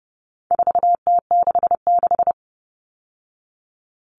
Morse code, heard as a keyed tone, about 30 wpm, 700 Hz. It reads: 4T66
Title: All Cut Numbers - 4-Digit 30wpm